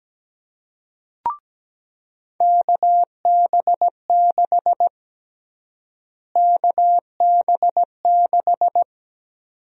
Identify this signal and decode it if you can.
Morse code, heard as a keyed tone, about 17 wpm, 700 Hz. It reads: KB6 KB6
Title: USA Call Sign Prefixes - Repeated 2x - 17wpm